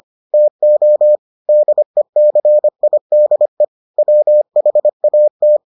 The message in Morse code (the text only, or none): TO DECIDE WHAT